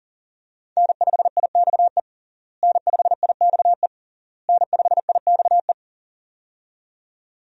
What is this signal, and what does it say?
Morse code, heard as a keyed tone, about 40 wpm, 700 Hz. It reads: N5IXE N5IXE N5IXE
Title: USA Call Signs - 5 Characters - Repeated 3x - 40wpm